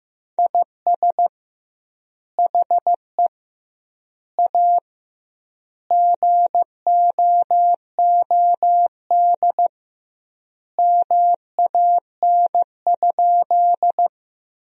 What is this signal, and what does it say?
Morse code, heard as a keyed tone, about 16 wpm, 700 Hz. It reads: IS HE A GOOD MAN?
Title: 2x Word-Spacing - Sentences from Top 200 Words - 15wpm